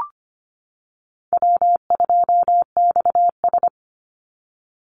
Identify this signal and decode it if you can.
Morse code, heard as a keyed tone, about 25 wpm, 700 Hz. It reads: W2XS
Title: Easy Call Signs - 25wpm